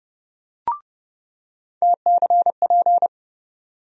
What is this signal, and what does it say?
Morse code, heard as a keyed tone, about 30 wpm, 700 Hz. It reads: TCP
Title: USA Call Sign Postfixes - 3 Letters 30wpm